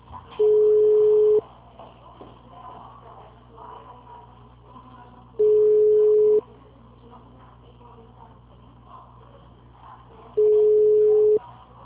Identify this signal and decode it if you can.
Ringback tone (Europe, 425 Hz).
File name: out- -2009- - - .7554